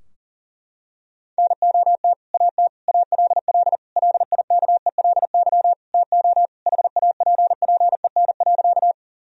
Morse code, heard as a keyed tone, about 40 wpm, 700 Hz. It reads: NOT AT ALL LIKELY TO HAPPEN.
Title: Sentences from Top 1000 Words - Review Part 4 - 40wpm